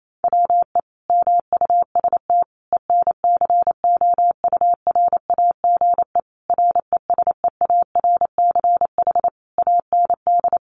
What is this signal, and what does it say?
Morse code, heard as a keyed tone, about 28 wpm, 700 Hz. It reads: WE MUST ENCOURAGE RESEARCH AND
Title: Sentences from Top 1400 Words 28wpm